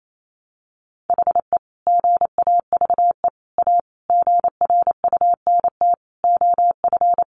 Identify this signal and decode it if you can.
Morse code, heard as a keyed tone, about 28 wpm, 700 Hz. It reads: HE GAVE A GRUNT OF
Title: A Sound Of Thunder 28wpm